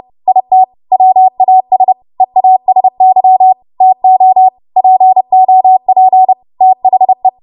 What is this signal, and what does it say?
Morse code, heard as a keyed tone, about 30 wpm, 750 Hz. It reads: IT WAS EASY TO POP THE
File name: Archive / 30 WPM / 30WPM